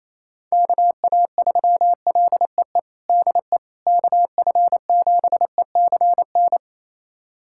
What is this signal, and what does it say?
Morse code, heard as a keyed tone, about 28 wpm, 700 Hz. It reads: KA3LEE DE KF7ECN
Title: USA Callsign DE Callsign - 6 Characters - 28wpm